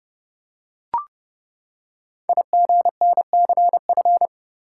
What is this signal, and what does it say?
Morse code, heard as a keyed tone, about 28 wpm, 700 Hz. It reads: IGNCF
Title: Code Groups - 5 Letters - 30wpm